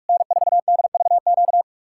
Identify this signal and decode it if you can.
Morse code, heard as a keyed tone, about 69 wpm, spaced as for 56 wpm, 700 Hz. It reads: N4DVQ